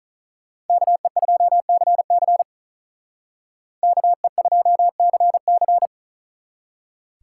Morse code, heard as a keyed tone, about 46 wpm, 700 Hz. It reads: KE2CC KE2CC